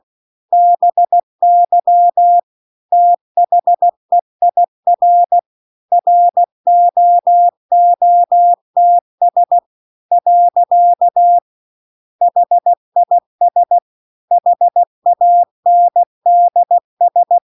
Morse code, heard as a keyed tone, about 16 wpm, 700 Hz. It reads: BY THEIR ROOTS . HIS HANDS